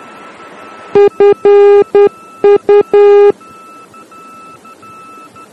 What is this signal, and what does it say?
Morse code, heard as a keyed tone, about 10 wpm, 410 Hz. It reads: FU
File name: germany / mx / FU-D